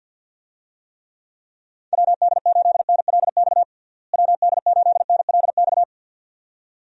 Morse code, heard as a keyed tone, about 45 wpm, 700 Hz. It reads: WD8NLX WD8NLX
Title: USA Call Signs - 6 Characters - Repeated 2x - 50wpm